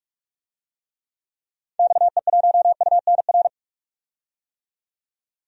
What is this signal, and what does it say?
Morse code, heard as a keyed tone, about 42 wpm, 700 Hz. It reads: XE1UNR
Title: DXCC-100 Call Signs - 6 Characters - 45wpm